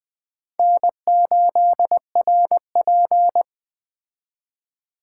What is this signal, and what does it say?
Morse code, heard as a keyed tone, about 21 wpm, 700 Hz. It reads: N8RP
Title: Speed Racing - USA 1x2 Call Signs 17wpm